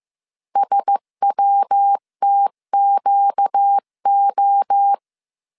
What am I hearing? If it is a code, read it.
Morse code, heard as a keyed tone, about 15 wpm, 800 Hz. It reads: SWTQO
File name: CW / DET / 1 9 1500C S B 14wpm 3302